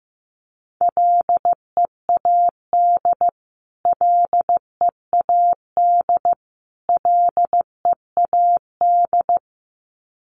Morse code, heard as a keyed tone, about 15 wpm, 700 Hz. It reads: LEAD LEAD LEAD
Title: Top 500 Words Repeated 15wpm